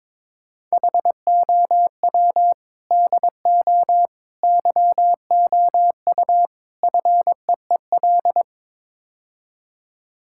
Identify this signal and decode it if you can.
Morse code, heard as a keyed tone, about 22 wpm, 700 Hz. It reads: HOW DO YOU FEEL
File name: practice / 22wpm / Build A Phrase - 5-Word Phrases - 22wpm